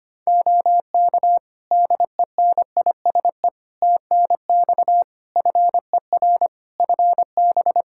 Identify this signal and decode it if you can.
Morse code, heard as a keyed tone, about 25 wpm, 700 Hz. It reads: OK DENISE TNX FER FB
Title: K1USN SST QSO Practice 25wpm